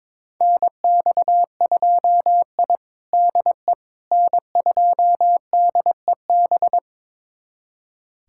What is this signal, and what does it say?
Morse code, heard as a keyed tone, about 22 wpm, 700 Hz. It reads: NX2I DE N2DEB
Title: USA Callsign DE Callsign 22wpm